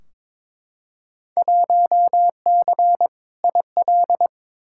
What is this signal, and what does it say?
Morse code, heard as a keyed tone, about 22 wpm, 700 Hz. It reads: 1C IL
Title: Field Day Exchanges - 22wpm